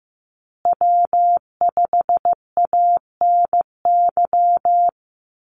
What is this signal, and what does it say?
Morse code, heard as a keyed tone, about 14 wpm, 700 Hz. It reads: W5ANY
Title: USA Call Signs Repeated 3x 15wpm